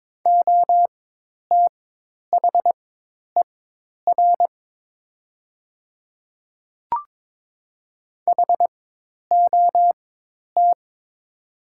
Morse code, heard as a keyed tone, about 22 wpm, spaced as for 10 wpm, 700 Hz. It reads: OTHER HOT
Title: 4x Character Spacing - Top 100 Words - 22wpm